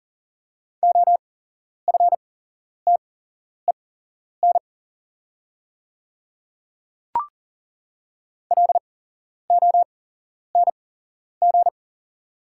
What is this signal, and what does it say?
Morse code, heard as a keyed tone, about 40 wpm, spaced as for 11 wpm, 690 Hz. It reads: OFTEN LONG
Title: 8x Character Spacing - Top 300 Words - 40wpm